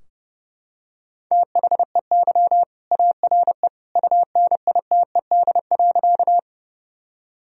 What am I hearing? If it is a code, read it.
Morse code, heard as a keyed tone, about 30 wpm, 700 Hz. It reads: THEY ARE UNITED.
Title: Sentences from Top 1000 Words - Review Part 4 - 30wpm